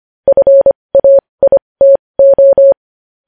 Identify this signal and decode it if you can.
Morse code, heard as a keyed tone, about 25 wpm, spaced as for 20 wpm, 550 Hz. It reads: FAITO
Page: news-headlines-in-morse-code-at-20-wpm